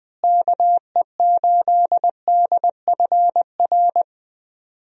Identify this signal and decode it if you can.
Morse code, heard as a keyed tone, about 20 wpm, 700 Hz. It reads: KE8DFR
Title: USA Call Signs - 6 Characters - Repeated 3x - 20wpm